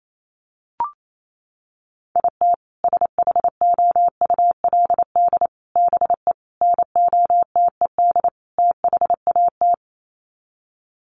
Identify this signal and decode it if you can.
Morse code, heard as a keyed tone, about 28 wpm, 700 Hz. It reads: IT SHOULD BE NOTED THAT
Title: Sets of 5 Words 28wpm